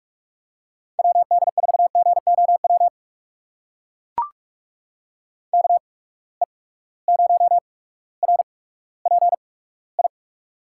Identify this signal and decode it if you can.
Morse code, heard as a keyed tone, about 43 wpm, spaced as for 19 wpm, 700 Hz. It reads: WD4GOW KE0RPI
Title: USA Call Signs - 6 Characters - 8x Character Spacing - 45wpm